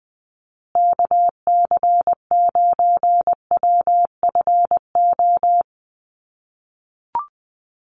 Morse code, heard as a keyed tone, about 20 wpm, 700 Hz. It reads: KC9WFO
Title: USA Call Signs Repeated 3x 20wpm